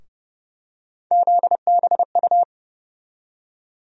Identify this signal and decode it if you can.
Morse code, heard as a keyed tone, about 30 wpm, 700 Hz. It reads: ZBU
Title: USA Call Sign Postfixes - 3 Letters 30wpm